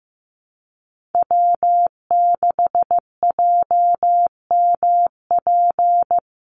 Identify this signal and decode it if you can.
Morse code, heard as a keyed tone, about 15 wpm, 700 Hz. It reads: W6JMP
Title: NA SOTA Chaser Callsigns 15wpm